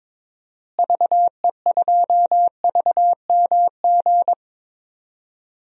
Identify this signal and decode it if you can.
Morse code, heard as a keyed tone, about 22 wpm, 700 Hz. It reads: VE2VMG
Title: DXCC-100 Call Signs - 6 Characters - Repeated 3x - 22wpm